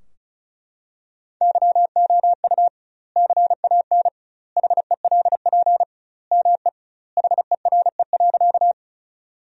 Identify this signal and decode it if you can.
Morse code, heard as a keyed tone, about 36 wpm, 700 Hz. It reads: YOU CAN HELP ME HERE.
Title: 2x Word-Spacing - Sentences from Top 200 Words - 35wpm